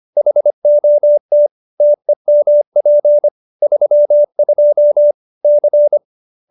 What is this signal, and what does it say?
Morse code, heard as a keyed tone, about 25 wpm, 590 Hz. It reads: HOT TEMP 32 C